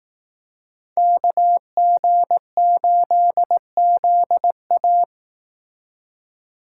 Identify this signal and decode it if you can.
Morse code, heard as a keyed tone, about 18 wpm, 700 Hz. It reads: KG8ZA
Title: Speed Racing - USA 2x2 Call Signs 15wpm